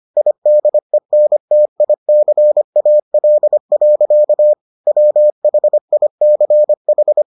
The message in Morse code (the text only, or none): IDENTICAL. WHICH